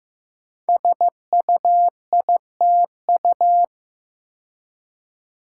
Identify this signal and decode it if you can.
Morse code, heard as a keyed tone, about 15 wpm, 700 Hz. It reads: SUITU
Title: Code Groups - 5 Letters - 15wpm